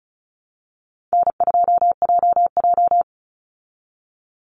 Morse code, heard as a keyed tone, about 34 wpm, 700 Hz. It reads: N2JJ
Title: Call Signs Repeated 3x - 35wpm